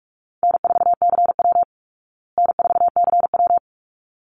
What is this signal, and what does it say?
Morse code, heard as a keyed tone, about 44 wpm, 700 Hz. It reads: N4CW N4CW
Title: Easy Call Signs Repeated 2x - 45wpm